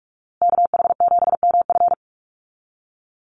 Morse code, heard as a keyed tone, about 44 wpm, 700 Hz. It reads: KH7MF
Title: USA Call Signs Repeated 2x 45wpm